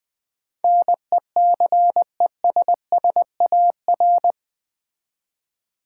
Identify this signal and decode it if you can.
Morse code, heard as a keyed tone, about 20 wpm, 700 Hz. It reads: NECESSAR
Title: Build A Word - 4 Letters At A Time - 11-Letter Words - 20wpm